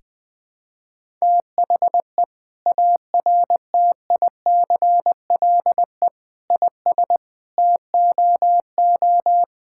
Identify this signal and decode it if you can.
Morse code, heard as a keyed tone, about 21 wpm, 700 Hz. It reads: THE ARTICLE IS TOO
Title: Sentences from Top 1000 Words - Review Part 5 - 20wpm